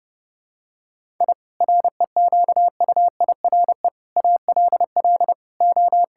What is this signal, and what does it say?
Morse code, heard as a keyed tone, about 30 wpm, 700 Hz. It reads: I REQUIRE ALL O
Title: Sentences from Top 1000 Words - Review Part 6 - 30wpm